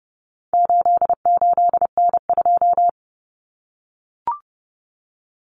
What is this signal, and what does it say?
Morse code, heard as a keyed tone, about 30 wpm, 700 Hz. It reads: 88N2
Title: Common Cut Numbers - 4-Digit 30wpm